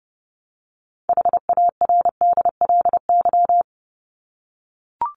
Morse code, heard as a keyed tone, about 29 wpm, 700 Hz. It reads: HARDLY
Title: LY Suffix - Top 100 Words with Base Words 30wpm